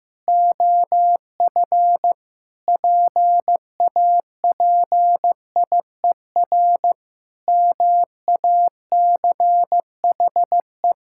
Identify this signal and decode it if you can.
Morse code, heard as a keyed tone, about 15 wpm, 700 Hz. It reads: OF PAPIER MACHE
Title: Pangram Challenge v2 - 15wpm